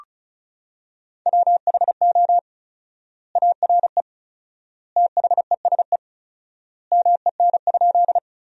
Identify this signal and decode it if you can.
Morse code, heard as a keyed tone, about 36 wpm, 700 Hz. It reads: WHO ARE THESE MEN?
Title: 4x Word-Spacing - Sentences from Top 200 Words - 35wpm